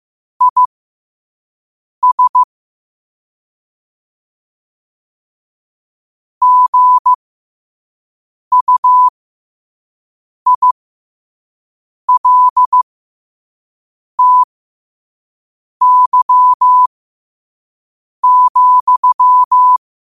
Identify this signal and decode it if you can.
Morse code, heard as a keyed tone, about 15 wpm, spaced as for 5 wpm, 1000 Hz. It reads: IS GUILTY,
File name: qotdfiles / 5wpm